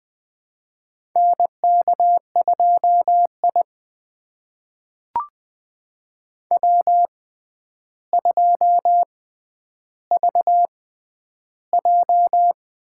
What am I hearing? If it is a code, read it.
Morse code, heard as a keyed tone, about 20 wpm, spaced as for 10 wpm, 700 Hz. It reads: NK2I W2VJ
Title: USA Call Signs - 4 Characters - 6x Character Spacing - 20wpm